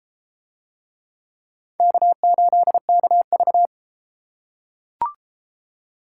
Morse code, heard as a keyed tone, about 32 wpm, 700 Hz. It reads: K8KV